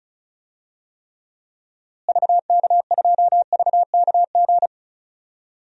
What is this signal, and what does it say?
Morse code, heard as a keyed tone, about 36 wpm, 700 Hz. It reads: VK2VKG
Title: DXCC-100 Call Signs - 6 Characters - Repeated 2x - 35wpm